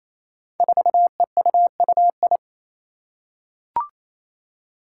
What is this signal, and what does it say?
Morse code, heard as a keyed tone, about 28 wpm, 700 Hz. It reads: 4EUUI
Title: Five Challenging Letter-Number Combinations 28wpm